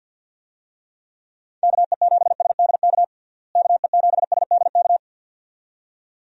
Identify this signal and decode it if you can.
Morse code, heard as a keyed tone, about 45 wpm, 700 Hz. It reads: KE7SDK KE7SDK
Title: USA Call Signs - 6 Characters - Repeated 2x - 50wpm